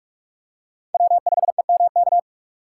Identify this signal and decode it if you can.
Morse code, heard as a keyed tone, about 43 wpm, 700 Hz. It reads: W5EMK
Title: USA Call Signs - 5 Characters - 3x Character Spacing - 45wpm